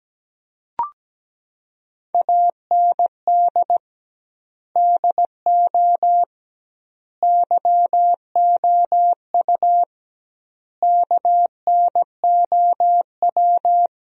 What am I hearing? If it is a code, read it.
Morse code, heard as a keyed tone, about 17 wpm, 700 Hz. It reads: AND DO YOU KNOW
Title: 2x Word-Spacing - Sentences from Top 200 Words - 17wpm